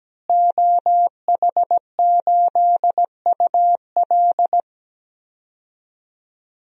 Morse code, heard as a keyed tone, about 17 wpm, 700 Hz. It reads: OH8UL
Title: DXCC-100 Call Signs - 5 Characters - Repeated 2x - 17wpm